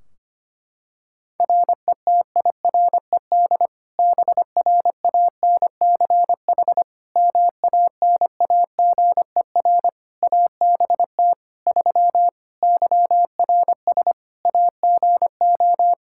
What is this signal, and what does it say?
Morse code, heard as a keyed tone, about 25 wpm, 700 Hz. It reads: RETIRED BRANCH MANAGER ABT 3 YRS AGO
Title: Practice QSO - Exchange 3 - More Info - 25wpm